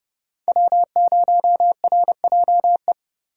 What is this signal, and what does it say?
Morse code, heard as a keyed tone, about 29 wpm, 700 Hz. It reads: W0RJE